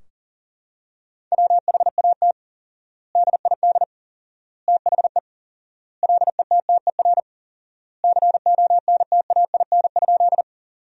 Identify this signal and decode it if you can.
Morse code, heard as a keyed tone, about 40 wpm, 700 Hz. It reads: WHAT DID THE LETTER CONTAIN?
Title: 4x Word-Spacing - Sentences from Top 500 Words - 40wpm